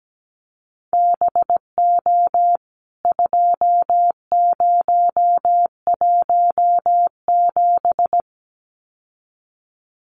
Morse code, heard as a keyed tone, about 17 wpm, 700 Hz. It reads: BO 2017
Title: CWT Exchange 17wpm